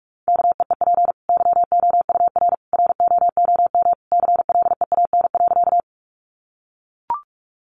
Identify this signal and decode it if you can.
Morse code, heard as a keyed tone, about 44 wpm, 700 Hz. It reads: KEEP YOUR ROOM CLEAN.
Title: Sentences from Top 1100 Words 45wpm